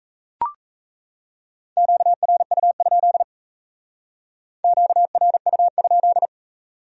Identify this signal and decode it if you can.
Morse code, heard as a keyed tone, about 35 wpm, 700 Hz. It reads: QRU? QRU?
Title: Speed-Racing Q-Codes - 28wpm